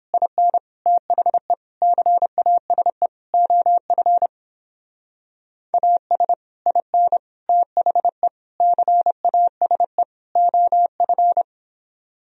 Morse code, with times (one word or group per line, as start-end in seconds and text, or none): 0.1-0.6 IN
0.9-1.5 THE
1.8-3.1 CASE
3.3-4.3 OF
5.7-6.3 AS
6.7-7.2 IN
7.5-8.3 THE
8.6-10.0 CASE
10.4-11.4 OF